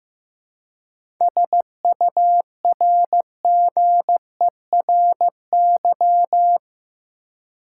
Morse code, heard as a keyed tone, about 15 wpm, 700 Hz. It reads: SURGERY
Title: Seven Letter Words - 15wpm